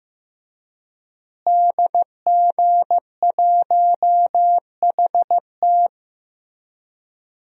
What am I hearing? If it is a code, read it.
Morse code, heard as a keyed tone, about 15 wpm, 700 Hz. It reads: DG1HT
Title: DXCC-100 Call Signs - 5 Characters - Repeated 3x - 15wpm